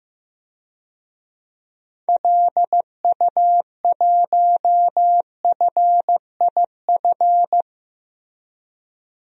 Morse code, heard as a keyed tone, about 15 wpm, 700 Hz. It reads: LU1FIF
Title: DXCC-100 Call Signs - 6 Characters - Repeated 2x - 15wpm